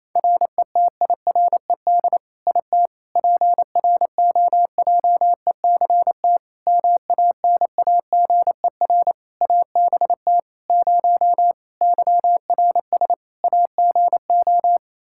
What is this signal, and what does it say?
Morse code, heard as a keyed tone, about 28 wpm, 700 Hz. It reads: RETIRED IT PROJECT MANAGER ABT 0 YRS AGO
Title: Practice QSO - Exchange 3 - More Info - 28wpm